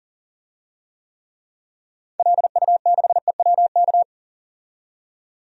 Morse code, heard as a keyed tone, about 40 wpm, 700 Hz. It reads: LU6EWK
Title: DXCC-100 Call Signs - 6 Characters - 40wpm